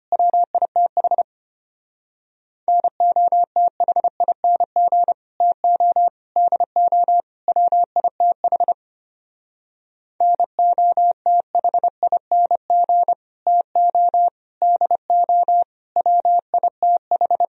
Morse code, heard as a keyed tone, about 29 wpm, 700 Hz. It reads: WITH NOTHING TO DO WITH NOTHING TO DO WITH
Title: Speed-Racing - Sets of 4 Words 25wpm